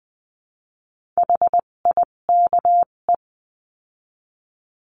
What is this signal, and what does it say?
Morse code, heard as a keyed tone, about 20 wpm, spaced as for 17 wpm, 700 Hz. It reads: HIKE
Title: 257 Lesson 42b - Words 14-20wpm